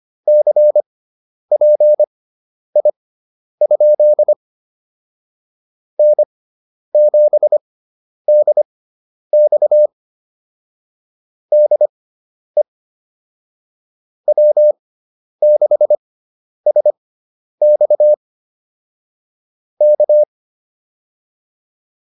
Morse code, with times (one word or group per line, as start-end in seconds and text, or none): 0.3-4.3 CPI?
6.0-9.9 N7DX
11.5-12.6 DE
14.3-18.1 W6SX
19.8-20.2 K